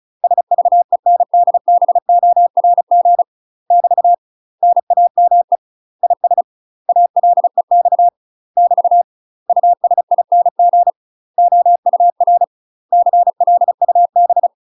Morse code, heard as a keyed tone, about 36 wpm, 700 Hz. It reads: SVENDBORG <BT> NAME IS ALEX <BT> USING OUR CLUB
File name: UK MASTER QSOs TRAINING / 35WPM / UK MASTER QSOs 0011